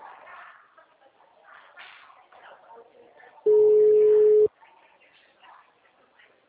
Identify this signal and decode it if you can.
Ringback tone (Europe, 425 Hz), with speech.